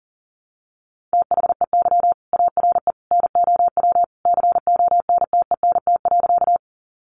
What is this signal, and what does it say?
Morse code, heard as a keyed tone, about 40 wpm, 700 Hz. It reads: THEY ARE NOW CONTENT.
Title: Sentences from Top 900 Words 40wpm